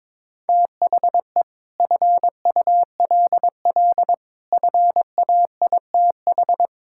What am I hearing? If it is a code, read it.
Morse code, heard as a keyed tone, about 22 wpm, 700 Hz. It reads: THE FULL FAITH
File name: practice / 22wpm / Sentences with 8-Word Phrases - 22wpm